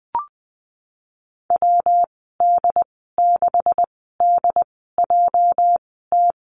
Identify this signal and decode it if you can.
Morse code, heard as a keyed tone, about 19 wpm, 700 Hz. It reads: W D 6 D J T
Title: 238 Lesson 38g - Letter q - Call Signs 20wpm